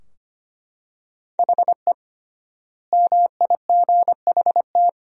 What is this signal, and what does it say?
Morse code, heard as a keyed tone, about 25 wpm, 700 Hz. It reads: HE MIGHT
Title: 3x Word-Spacing - Sentences from Top 300 Words - 25wpm